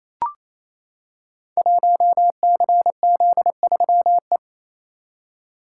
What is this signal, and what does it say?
Morse code, heard as a keyed tone, about 28 wpm, 700 Hz. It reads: 1CZ3E
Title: Code Groups - 5 Letter:Numbers - 28wpm